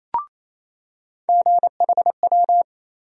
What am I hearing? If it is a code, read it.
Morse code, heard as a keyed tone, about 28 wpm, 700 Hz. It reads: GHW
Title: USA Call Sign Postfixes - 3 Letters 28wpm